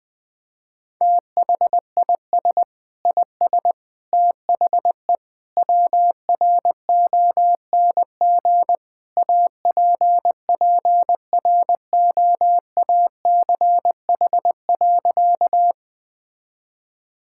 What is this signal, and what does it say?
Morse code, heard as a keyed tone, about 21 wpm, 700 Hz. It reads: THIS IS THE WRONG APPROACH.
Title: Sentences from Top 1000 Words - Review Part 6 - 20wpm